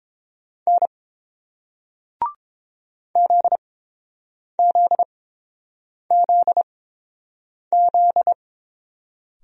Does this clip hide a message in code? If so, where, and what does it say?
Yes: Morse code from 0.7 to 8.3 s, heard as a keyed tone, about 31 wpm, spaced as for 7 wpm, 700 Hz. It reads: N ZZZZ